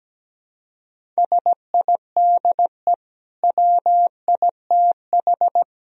Morse code, heard as a keyed tone, about 17 wpm, 700 Hz. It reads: SIDE WITH
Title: Sentences from Top 1000 Words - Review Part 1 - 17wpm